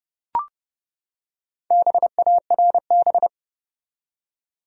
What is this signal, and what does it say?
Morse code, heard as a keyed tone, about 32 wpm, 700 Hz. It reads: BARB
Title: USA Common Names - Max 4 Letters - 30wpm